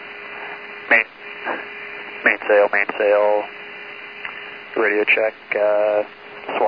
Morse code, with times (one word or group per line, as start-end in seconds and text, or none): none